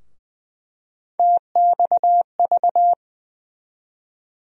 Morse code, heard as a keyed tone, about 20 wpm, 700 Hz. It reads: TXV
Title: USA Call Sign Postfixes - 3 Letters 20wpm